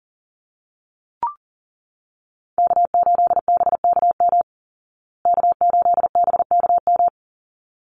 Morse code, heard as a keyed tone, about 40 wpm, 700 Hz. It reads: K8BKM K8BKM
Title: Call Signs Repeated 3x - 40wpm